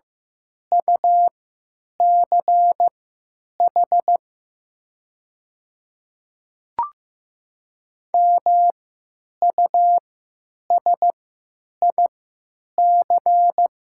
Morse code, heard as a keyed tone, about 15 wpm, spaced as for 9 wpm, 700 Hz. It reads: UCH MUSIC
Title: 3x Character Spacing - Top 300 Words - 15wpm